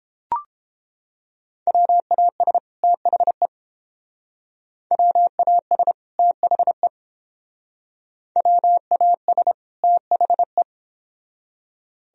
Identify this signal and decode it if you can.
Morse code, heard as a keyed tone, about 28 wpm, 700 Hz. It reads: WAS THE WAS THE WAS THE